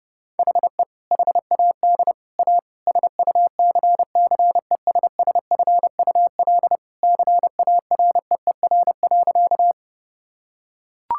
Morse code, heard as a keyed tone, about 30 wpm, 700 Hz. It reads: HE HAD A SUCCESSFUL CAREER.
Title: Sentences from Top 1000 Words - Review Part 6 - 30wpm